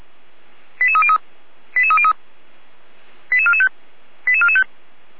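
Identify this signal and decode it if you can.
Five-tone selective call (ZVEI), heard as quick runs of five tones, tones of about 70 ms. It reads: 88303 88303 88445 88445
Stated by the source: Five-tone ZVEI selective call 88261, received on an SDR.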